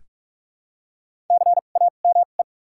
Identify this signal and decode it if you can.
Morse code, heard as a keyed tone, about 45 wpm, spaced as for 32 wpm, 700 Hz. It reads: CAME
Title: 2x Character Spacing - Top 200 Words - 45wpm